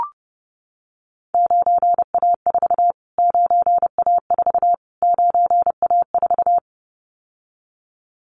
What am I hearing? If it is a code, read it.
Morse code, heard as a keyed tone, about 30 wpm, 700 Hz. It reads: 9A4 9A4 9A4
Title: Callsign Prefixes for Top 25 DXCC - Repeated 3x - 30wpm